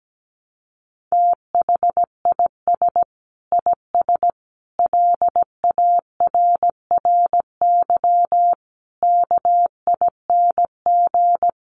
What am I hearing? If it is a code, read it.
Morse code, heard as a keyed tone, about 17 wpm, 700 Hz. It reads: THIS IS LARRY KING
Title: Sets of 5 Words Encore2 17wpm